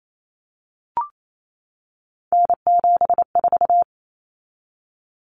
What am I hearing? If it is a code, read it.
Morse code, heard as a keyed tone, about 28 wpm, 700 Hz. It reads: N74
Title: Common Cut Numbers - 3-Digit 28wpm